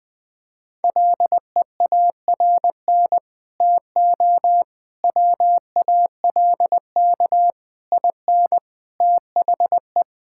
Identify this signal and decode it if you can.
Morse code, heard as a keyed tone, about 21 wpm, 700 Hz. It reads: LEARN TO WALK IN THE
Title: Sets of 9 Words - 20wpm